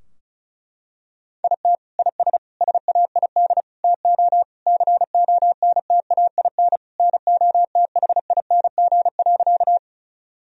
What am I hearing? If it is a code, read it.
Morse code, heard as a keyed tone, about 36 wpm, 700 Hz. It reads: IT IS SAID TO CONTAIN NOTHING.
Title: Sentences from Top 1000 Words - Review Part 2 - 35wpm